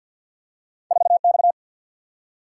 Morse code, heard as a keyed tone, about 47 wpm, 700 Hz. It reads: 4X
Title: Single-Letter Single-Number 50wpm